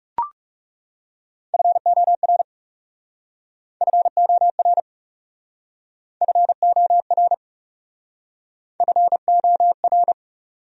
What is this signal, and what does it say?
Morse code, heard as a keyed tone, about 39 wpm, 700 Hz. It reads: FOR FOR FOR FOR